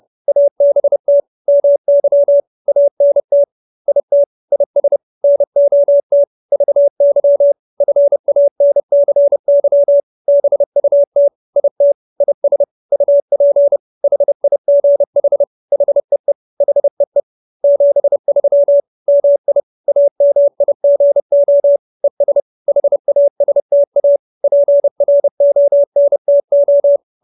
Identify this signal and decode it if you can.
Morse code, heard as a keyed tone, about 30 wpm, 570 Hz. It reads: ABT MY ANT IT IS NOT VY FANCY BUT IT IS UP HIGH HEE HEE 73 MI AMIGO ES HASTA PRONTO
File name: QSO317 30